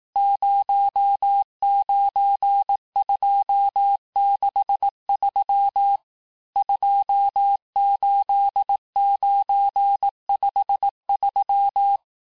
Morse code, heard as a keyed tone, about 18 wpm, 800 Hz. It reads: 09263 28953
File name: cours-du-lundi files / cours-1852-texte-chiffre-vitesse-17MPM